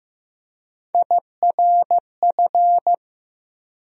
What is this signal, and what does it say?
Morse code, heard as a keyed tone, about 15 wpm, 700 Hz. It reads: IRF
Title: USA Call Sign Postfixes - 3 Letters 15wpm